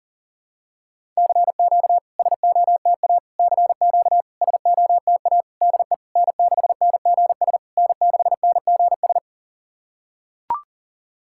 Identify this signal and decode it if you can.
Morse code, heard as a keyed tone, about 40 wpm, 700 Hz. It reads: CQ SOTA CQ SOTA DE N6NGS N6NGS
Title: SOTA QSO Practice 40wpm